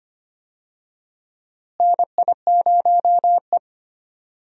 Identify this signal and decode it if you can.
Morse code, heard as a keyed tone, about 25 wpm, 700 Hz. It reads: NI0E
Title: USA Call Signs - 4 Characters - Repeated 2x - 25wpm